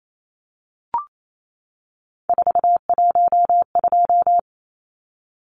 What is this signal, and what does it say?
Morse code, heard as a keyed tone, about 28 wpm, 700 Hz. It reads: 412
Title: Numbers - 3-Digit 28wpm